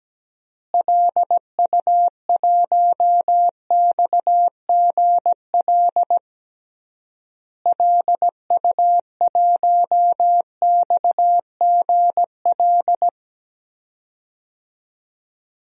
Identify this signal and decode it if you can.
Morse code, heard as a keyed tone, about 17 wpm, 700 Hz. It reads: LU1XGL LU1XGL
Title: DXCC-100 Call Signs - 6 Characters - Repeated 3x - 17wpm